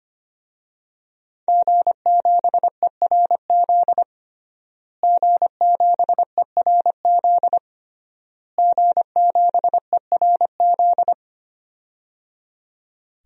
Morse code, heard as a keyed tone, about 25 wpm, 700 Hz. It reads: G7ERZ G7ERZ G7ERZ
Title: DXCC-100 Call Signs - 5 Characters - Repeated 3x - 25wpm